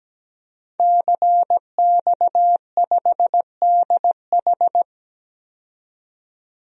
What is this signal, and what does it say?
Morse code, heard as a keyed tone, about 17 wpm, 700 Hz. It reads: CX5DH
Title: DXCC-100 Call Signs - 5 Characters - 17wpm